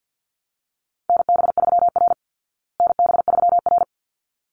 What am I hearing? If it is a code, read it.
Morse code, heard as a keyed tone, about 48 wpm, 700 Hz. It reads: NB3R NB3R
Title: Call Signs Repeated 3x - 50wpm